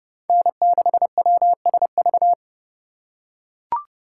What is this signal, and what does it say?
Morse code, heard as a keyed tone, about 27 wpm, 700 Hz. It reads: N6WSV